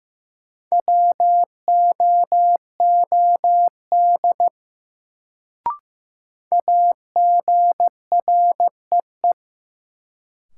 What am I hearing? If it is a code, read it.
Morse code, heard as a keyed tone, about 15 wpm, 700 Hz. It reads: WOOD AGREE
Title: Sentences from Top 1100 Words - New Word Drill Alt - 15wpm